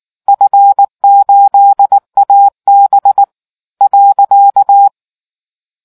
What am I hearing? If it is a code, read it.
Morse code, heard as a keyed tone, about 19 wpm, 800 Hz. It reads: F8AB .